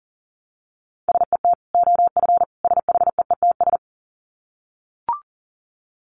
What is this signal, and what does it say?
Morse code, heard as a keyed tone, about 40 wpm, 700 Hz. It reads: SET OF SHEETS
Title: Dit Challenge 40wpm